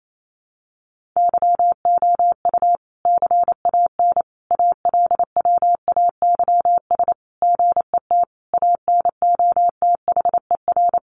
Morse code, heard as a keyed tone, about 28 wpm, 700 Hz. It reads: YOU CAN ALWAYS GET ANOTHER
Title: Sentences from Top 1200 Words 28wpm